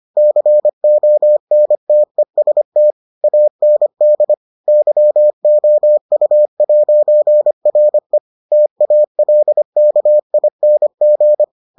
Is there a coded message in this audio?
Yes: Morse code from 0.2 to 11.4 s, heard as a keyed tone, about 25 wpm, 600 Hz. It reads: CONTEST AND YOU'RE TALKING